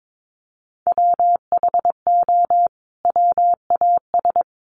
Morse code, heard as a keyed tone, about 22 wpm, 700 Hz. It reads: WHO WAS
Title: Sets of 2 Words 22wpm